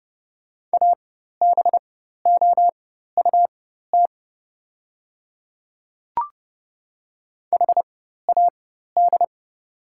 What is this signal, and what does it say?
Morse code, heard as a keyed tone, about 30 wpm, spaced as for 14 wpm, 700 Hz. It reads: ABOUT HAD
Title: 4x Character Spacing - Top 300 Words - 30wpm